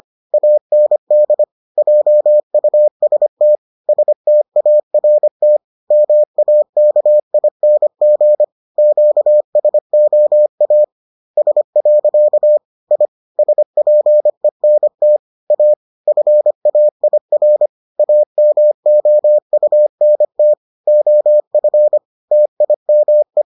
Morse code, heard as a keyed tone, about 25 wpm, 600 Hz. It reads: AND JUST START MAKING QSOAS. I SPENT A FAIR AMOUNT OF TIME